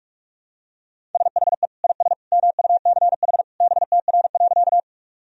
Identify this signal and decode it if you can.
Morse code, heard as a keyed tone, about 47 wpm, 700 Hz. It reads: SHE IS MUCH BTR.
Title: Sentences from Top 400 Words and Abbreviations 45wpm